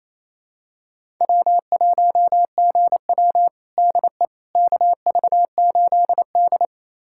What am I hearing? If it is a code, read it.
Morse code, heard as a keyed tone, about 28 wpm, 700 Hz. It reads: W1GW DE KV8D
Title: USA Callsign DE Callsign - 4 Characters - 28wpm